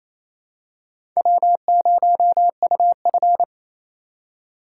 Morse code, heard as a keyed tone, about 28 wpm, 700 Hz. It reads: W0UF
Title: K1USN SST QSO Practice - 28wpm